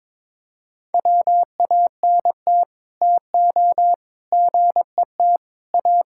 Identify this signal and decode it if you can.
Morse code, heard as a keyed tone, about 22 wpm, 700 Hz. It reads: WANT TO GET A
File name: Sets of 9 Words - 22wpm